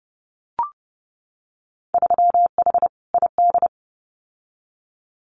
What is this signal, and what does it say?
Morse code, heard as a keyed tone, about 32 wpm, 700 Hz. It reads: 3H ID
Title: Winter Field Day Exchanges Encore 30wpm